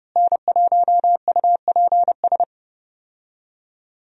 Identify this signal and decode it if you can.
Morse code, heard as a keyed tone, about 30 wpm, 700 Hz. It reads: N1UPS
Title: USA Call Signs - 5 Characters - Repeated 3x - 30wpm